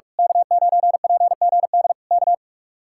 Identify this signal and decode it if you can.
Morse code, heard as a keyed tone, about 41 wpm, 700 Hz. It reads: K9PGD K
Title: Speed-Racing Common QSO Phrases Encore - 45wpm